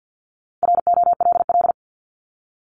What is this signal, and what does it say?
Morse code, heard as a keyed tone, about 50 wpm, 700 Hz. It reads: ROLL
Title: Multi-letter Phonemes Repeated 3x 50wpm